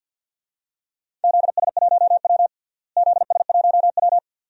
Morse code, heard as a keyed tone, about 52 wpm, 700 Hz. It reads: ZS1W ZS1W
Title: DXCC-100 Call Signs - 4 Characters - Repeated 2x - 50wpm